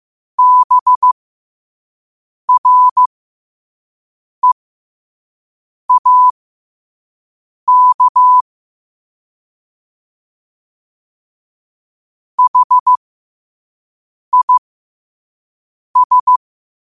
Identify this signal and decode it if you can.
Morse code, heard as a keyed tone, about 15 wpm, spaced as for 5 wpm, 1000 Hz. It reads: BREAK HIS
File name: qotdfiles / 5wpm